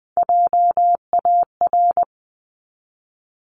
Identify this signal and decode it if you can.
Morse code, heard as a keyed tone, about 20 wpm, 700 Hz. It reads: JAR
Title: 260 Lesson 42e - Words 20-20wpm